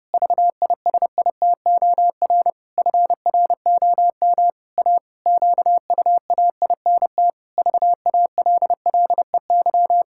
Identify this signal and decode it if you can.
Morse code, heard as a keyed tone, about 30 wpm, 700 Hz. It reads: VISITOR FROM A QUAINT VALLEY
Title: Pangram Challenge v2 - 30wpm